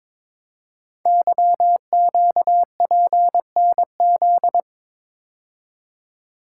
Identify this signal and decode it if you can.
Morse code, heard as a keyed tone, about 23 wpm, 700 Hz. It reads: YQPNZ
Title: Code Groups - 5 Letters - 22wpm